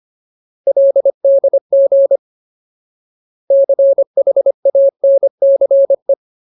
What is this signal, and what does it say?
Morse code, heard as a keyed tone, about 25 wpm, 550 Hz. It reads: LDG CHANCE
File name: Ultimate Word List / 25WPM Ultimate